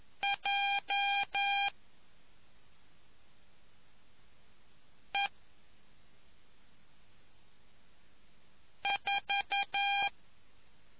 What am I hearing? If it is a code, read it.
Morse code, heard as a keyed tone, about 11 wpm, 800 Hz. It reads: J E 4